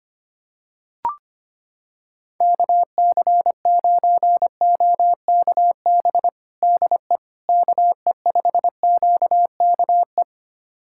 Morse code, heard as a keyed tone, about 25 wpm, 700 Hz. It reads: KC9OKB DE KE5QKE
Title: USA Callsign DE Callsign - 6 Characters - 25wpm